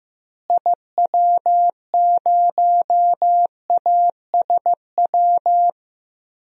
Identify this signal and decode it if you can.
Morse code, heard as a keyed tone, about 15 wpm, 700 Hz. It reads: IW0ASW
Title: DXCC-100 Call Signs - 6 Characters - Repeated 2x - 15wpm